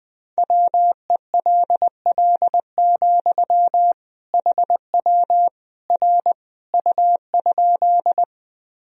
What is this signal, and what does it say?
Morse code, heard as a keyed tone, about 21 wpm, 700 Hz. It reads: WELL, HW R U?
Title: Sentences from Top 200 Words and Abbreviations 20wpm